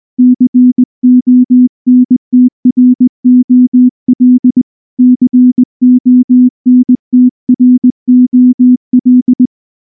Morse code, heard as a keyed tone, about 19 wpm, 250 Hz. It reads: CONTROL CONTROL
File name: morse / morse-training-20WPM-10